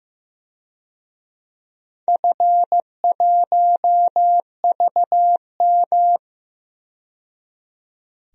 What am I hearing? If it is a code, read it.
Morse code, heard as a keyed tone, about 15 wpm, 700 Hz. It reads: F1VM
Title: DXCC-100 Call Signs - 4 Characters - 15wpm